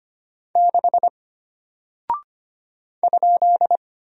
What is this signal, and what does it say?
Morse code, heard as a keyed tone, about 25 wpm, 700 Hz. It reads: 6 ?
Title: Single Character (Letter:Number:Punctuation:Prosign) Alt 25wpm